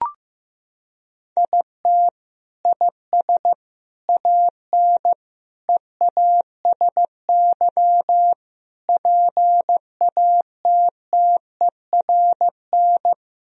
Morse code, heard as a keyed tone, about 16 wpm, 700 Hz. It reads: IT IS AN EASY PATTERN
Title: Sentences from Top 1000 Words - Review Part 2 - 15wpm